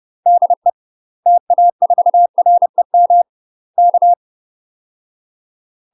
Morse code, heard as a keyed tone, about 30 wpm, 700 Hz. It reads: DE TA4REM K
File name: UK MASTER QSOs TRAINING / 30WPM / UK MASTER QSOs 0008